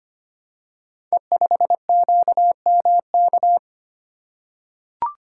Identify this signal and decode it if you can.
Morse code, heard as a keyed tone, about 25 wpm, 700 Hz. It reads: E5QMK
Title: Code Groups - 5 Letter:Numbers - 25wpm